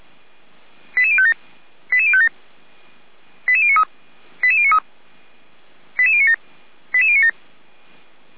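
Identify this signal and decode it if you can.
Five-tone selective call (ZVEI), heard as quick runs of five tones, tones of about 70 ms. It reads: 88057 88057 88093 88093 88097 88097
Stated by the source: ZVEI five-tone call received on SDR, from the audio archive of the Orlamünde fire brigade (feuerwehr-orlamuende).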